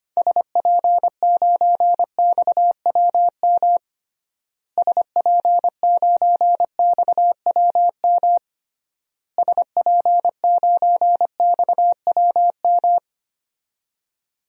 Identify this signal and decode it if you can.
Morse code, heard as a keyed tone, about 25 wpm, 700 Hz. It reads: SP9XWM SP9XWM SP9XWM
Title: DXCC-100 Call Signs - 6 Characters - Repeated 3x - 25wpm